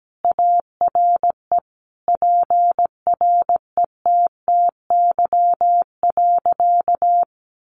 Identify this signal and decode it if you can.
Morse code, heard as a keyed tone, about 17 wpm, 700 Hz. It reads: ARE PRETTY.
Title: Additional Easy Phrases 17wpm